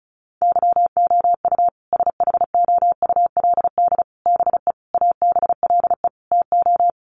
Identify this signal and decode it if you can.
Morse code, heard as a keyed tone, about 34 wpm, 700 Hz. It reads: YOU SHOULD BE ABLE TO
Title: Sets of 5 Words Encore 35wpm